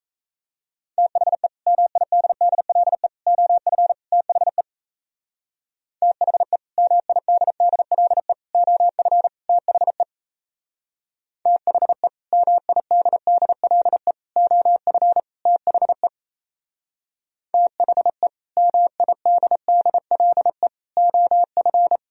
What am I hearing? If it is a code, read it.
Morse code, heard as a keyed tone, about 36 wpm, 700 Hz. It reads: THE MIDDLE OF THE THE MIDDLE OF THE THE MIDDLE OF THE THE MIDDLE OF